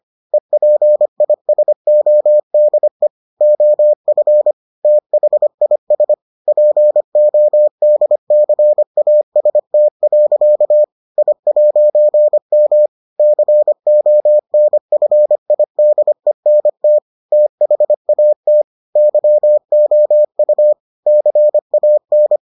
Morse code, heard as a keyed tone, about 25 wpm, 600 Hz. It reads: EPISODE OF THIS PODCAST. I'M CONFIDENT THAT YOU CAN